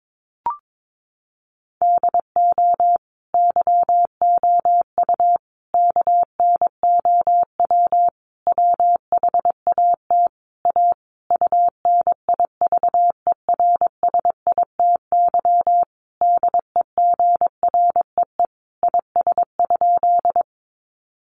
Morse code, heard as a keyed tone, about 22 wpm, 700 Hz. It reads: DO YOU KNOW WHAT A UNIVERSITY DEGREE IS?
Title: Sentences from Top 1500 Words 22wpm